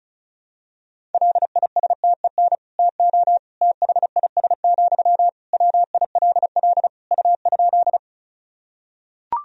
Morse code, heard as a keyed tone, about 36 wpm, 700 Hz. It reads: LISTEN TO THIS, WILL U?
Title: Sentences from Top 400 Words and Abbreviations 35wpm